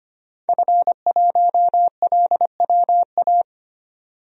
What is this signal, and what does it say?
Morse code, heard as a keyed tone, about 25 wpm, 700 Hz. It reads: F1LWA
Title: DXCC-100 Call Signs - 5 Characters - 25wpm